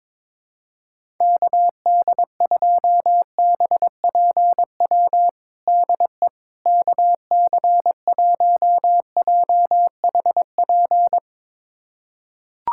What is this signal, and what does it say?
Morse code, heard as a keyed tone, about 22 wpm, 700 Hz. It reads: KD2BPW DE KC1JHP
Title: USA Callsign DE Callsign 22wpm